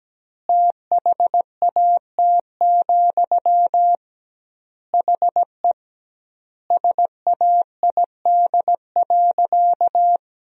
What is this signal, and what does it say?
Morse code, heard as a keyed tone, about 17 wpm, 700 Hz. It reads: THAT, HE SAID.
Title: 2x Word-Spacing - Sentences from Top 200 Words - 17wpm